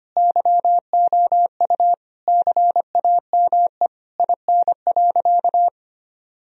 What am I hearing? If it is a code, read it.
Morse code, heard as a keyed tone, about 25 wpm, 700 Hz. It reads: YOU CAME IN.
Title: Sentences from Top 1000 Words - Review Part 3 - 25wpm